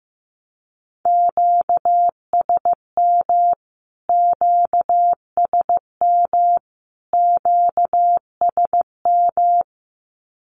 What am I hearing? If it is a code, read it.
Morse code, heard as a keyed tone, about 15 wpm, 700 Hz. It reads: QSM QSM QSM
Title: Q-Codes Repeated 3x 15wpm